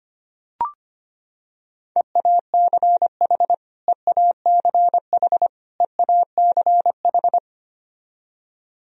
Morse code, heard as a keyed tone, about 25 wpm, 700 Hz. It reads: EACH EACH EACH
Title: Top 100 Words Repeated 25wpm